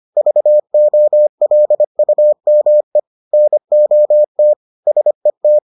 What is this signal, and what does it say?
Morse code, heard as a keyed tone, about 25 wpm, 600 Hz. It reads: VOLUME NOT SET